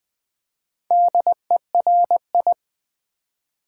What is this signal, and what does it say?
Morse code, heard as a keyed tone, about 20 wpm, 700 Hz. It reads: DERI
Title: Build A Word - 2 Letters At A Time - 6-Letter Words - 20wpm